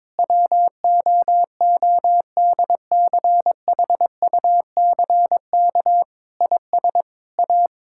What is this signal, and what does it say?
Morse code, heard as a keyed tone, about 22 wpm, 700 Hz. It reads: WOODCHUCK IS A
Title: Pangram Challenge v2 - 22wpm